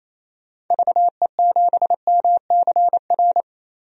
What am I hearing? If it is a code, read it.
Morse code, heard as a keyed tone, about 28 wpm, 700 Hz. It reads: VE7MCR
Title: DXCC-100 Call Signs - 6 Characters - Repeated 3x - 28wpm